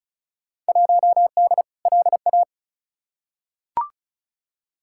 Morse code, heard as a keyed tone, about 36 wpm, 700 Hz. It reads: 1D LA
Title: Field Day Exchanges - 35wpm